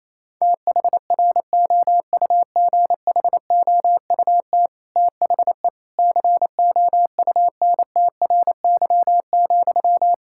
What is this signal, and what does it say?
Morse code, heard as a keyed tone, about 28 wpm, 700 Hz. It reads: THROUGHOUT THE COUNTRY,
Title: Sentences from Top 1000 Words - Review Part 5 - 28wpm